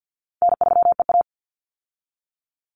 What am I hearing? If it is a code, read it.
Morse code, heard as a keyed tone, about 50 wpm, 700 Hz. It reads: N3EA
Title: Easy Call Signs Repeated 2x - 50wpm